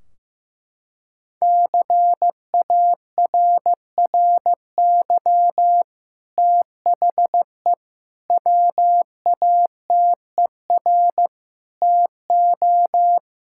Morse code, heard as a keyed tone, about 15 wpm, 700 Hz. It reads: CARRY THE WATER TO
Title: Sentences from Top 1600 Words 15wpm